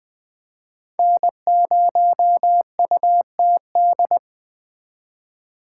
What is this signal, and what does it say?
Morse code, heard as a keyed tone, about 20 wpm, 700 Hz. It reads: N0UTD